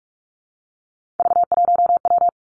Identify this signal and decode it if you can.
Morse code, heard as a keyed tone, about 46 wpm, 700 Hz. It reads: V1W
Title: Three Challenging Letter-Number Combinations 45wpm